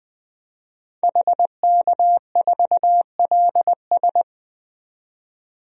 Morse code, heard as a keyed tone, about 21 wpm, 700 Hz. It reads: HK4LS
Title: DXCC-100 Call Signs - 5 Characters - 20wpm